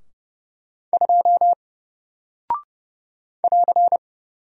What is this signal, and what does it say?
Morse code, heard as a keyed tone, about 30 wpm, 700 Hz. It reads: 2 <AR>
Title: Single Character (Letter:Number:Punctuation:Prosign) Alt 30wpm